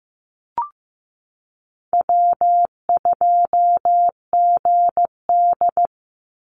Morse code, heard as a keyed tone, about 15 wpm, 700 Hz. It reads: W2GD
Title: Easy Call Signs Repeated 3x - 15wpm